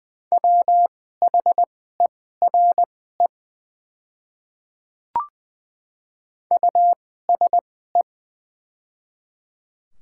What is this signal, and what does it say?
Morse code, heard as a keyed tone, about 20 wpm, spaced as for 14 wpm, 700 Hz. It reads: WHERE USE
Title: 2x Character Spacing - Top 200 Words - 20wpm